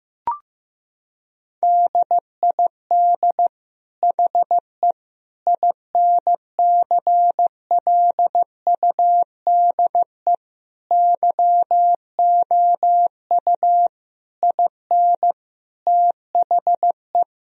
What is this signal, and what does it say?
Morse code, heard as a keyed tone, about 15 wpm, 700 Hz. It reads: DID HE INCLUDE YOU IN THE
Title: Sentences from Top 1000 Words - Review Part 4 - 15wpm